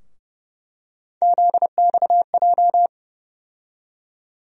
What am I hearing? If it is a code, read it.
Morse code, heard as a keyed tone, about 30 wpm, 700 Hz. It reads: ZXJ
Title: USA Call Sign Postfixes - 3 Letters 30wpm